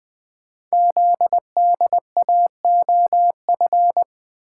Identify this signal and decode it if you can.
Morse code, heard as a keyed tone, about 20 wpm, 700 Hz. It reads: ZDAOF
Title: Code Groups - 5 Letter:Numbers - 20wpm